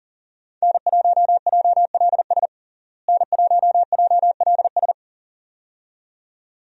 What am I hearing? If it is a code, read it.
Morse code, heard as a keyed tone, about 37 wpm, 700 Hz. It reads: N1JLS N1JLS
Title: USA Call Signs - 5 Characters - Repeated 2x - 40wpm